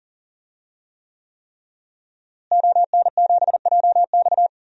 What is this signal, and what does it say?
Morse code, heard as a keyed tone, about 40 wpm, 700 Hz. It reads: ON7JX
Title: DXCC-100 Call Signs - 5 Characters - Repeated 3x - 40wpm